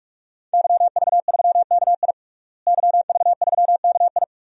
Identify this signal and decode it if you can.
Morse code, heard as a keyed tone, about 45 wpm, 700 Hz. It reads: YV3KI YV3KI
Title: DXCC-100 Call Signs - 5 Characters - Repeated 2x - 45wpm